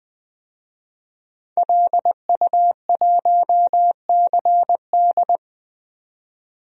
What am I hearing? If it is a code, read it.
Morse code, heard as a keyed tone, about 20 wpm, 700 Hz. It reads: LU1CD
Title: DXCC-100 Call Signs - 5 Characters - 20wpm